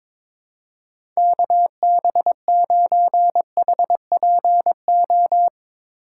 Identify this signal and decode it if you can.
Morse code, heard as a keyed tone, about 22 wpm, 700 Hz. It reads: KB9HPO
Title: USA Call Signs - 6 Characters - 3x Character Spacing - 22wpm